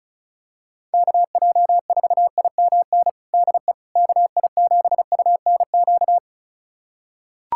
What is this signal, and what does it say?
Morse code, heard as a keyed tone, about 36 wpm, 700 Hz. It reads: KJ4IMN DE KI7UNQ
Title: USA Callsign DE Callsign - 6 Characters - 35wpm